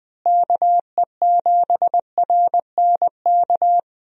Morse code, heard as a keyed tone, about 20 wpm, 700 Hz. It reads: KE7RNK
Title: USA Call Signs - 6 Characters - Repeated 2x - 20wpm